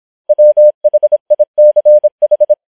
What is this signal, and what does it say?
Morse code, heard as a keyed tone, about 26 wpm, 600 Hz. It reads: WHICH